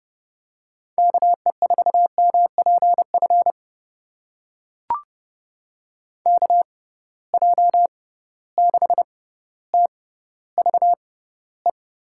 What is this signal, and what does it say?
Morse code, heard as a keyed tone, about 30 wpm, spaced as for 15 wpm, 700 Hz. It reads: KE4MPF KJ6TVE
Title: USA Call Signs - 6 Characters - 6x Character Spacing - 30wpm